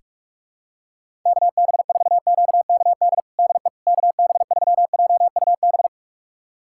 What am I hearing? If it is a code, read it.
Morse code, heard as a keyed tone, about 41 wpm, 700 Hz. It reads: KB4QKD DE KB3JUB
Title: USA Callsign DE Callsign - 6 Characters - 45wpm